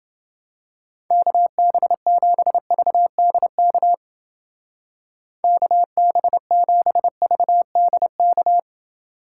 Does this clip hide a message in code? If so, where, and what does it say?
Yes: Morse code from 1.1 to 8.6 s, heard as a keyed tone, about 29 wpm, 700 Hz. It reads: KB7VDK KB7VDK